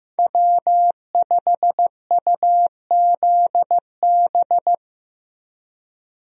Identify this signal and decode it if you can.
Morse code, heard as a keyed tone, about 15 wpm, 700 Hz. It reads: W5UZB
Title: USA Call Signs - 5 Characters - Repeated 3x - 15wpm